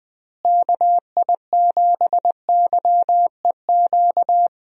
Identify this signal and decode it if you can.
Morse code, heard as a keyed tone, about 21 wpm, 700 Hz. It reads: KI7YEQ
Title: USA Call Signs - 6 Characters - Repeated 3x - 20wpm